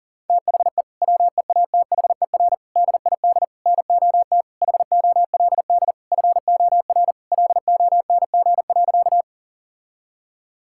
Morse code, heard as a keyed tone, about 37 wpm, 700 Hz. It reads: THE WEATHER DID NOT HOLD FOR LONG.
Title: Sentences from Top 1000 Words - Review Part 5 - 40wpm